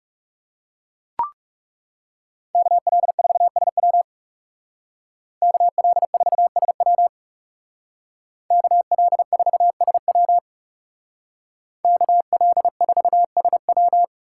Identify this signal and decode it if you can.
Morse code, heard as a keyed tone, about 39 wpm, 700 Hz. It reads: KL4SW KL4SW KL4SW KL4SW